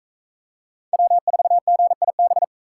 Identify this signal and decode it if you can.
Morse code, heard as a keyed tone, about 44 wpm, 700 Hz. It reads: W4GIB